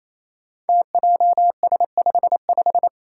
Morse code, heard as a keyed tone, about 27 wpm, 700 Hz. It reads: TJS55
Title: Five Challenging Letter-Number Combinations 28wpm